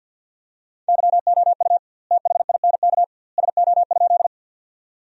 Morse code, heard as a keyed tone, about 53 wpm, 700 Hz. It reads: YOU THINK SO?
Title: 2x Word-Spacing - Sentences from Top 200 Words - 50wpm